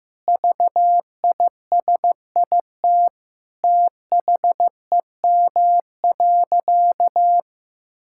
Morse code, heard as a keyed tone, about 15 wpm, 700 Hz. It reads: VISIT THEM.
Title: Sentences from Top 1000 Words - Review Part 6 - 15wpm